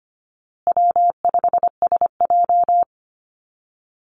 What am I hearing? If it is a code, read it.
Morse code, heard as a keyed tone, about 25 wpm, 700 Hz. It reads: W5SJ
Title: Easy Call Signs Repeated 3x - 25wpm